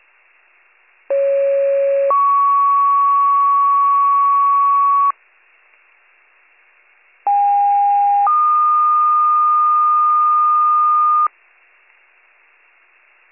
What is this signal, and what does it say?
Two-tone page, heard as a one-second tone, then a three-second tone, about 550 then 1120 Hz, about 810 then 1220 Hz. It is heard from 1.1 to 11.3 s.